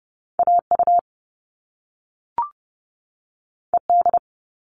Morse code, heard as a keyed tone, about 30 wpm, 700 Hz. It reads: A U E D
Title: All Cut Numbers - 2-Digit 30wpm